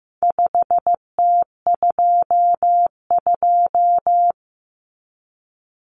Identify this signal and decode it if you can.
Morse code, heard as a keyed tone, about 15 wpm, 700 Hz. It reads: 5T22
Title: Common Cut Numbers - 4-Digit 15wpm